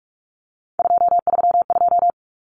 Morse code, heard as a keyed tone, about 44 wpm, 700 Hz. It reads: 232
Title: Common Cut Numbers - 3-Digit 45wpm